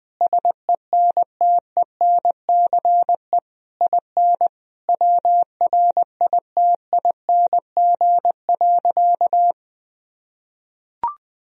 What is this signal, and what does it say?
Morse code, heard as a keyed tone, about 20 wpm, 700 Hz. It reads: SENTENCE IN WRITING.
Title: Sentences from Top 1000 Words - Review Part 1 - 20wpm